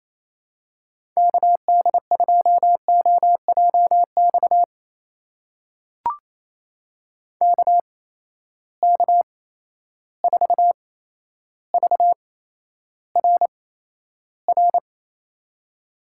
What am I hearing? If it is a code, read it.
Morse code, heard as a keyed tone, about 28 wpm, spaced as for 12 wpm, 700 Hz. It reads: KD2OJX KK4VRR
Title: USA Call Signs - 6 Characters - 8x Character Spacing - 28wpm